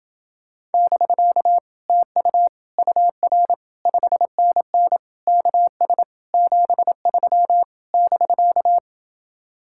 Morse code, heard as a keyed tone, about 30 wpm, 700 Hz. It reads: <BK> TU UR 5NN KS 73 <BK>